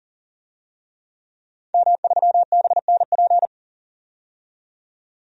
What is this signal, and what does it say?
Morse code, heard as a keyed tone, about 38 wpm, 700 Hz. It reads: M3BNP
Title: DXCC-100 Call Signs - 5 Characters - 40wpm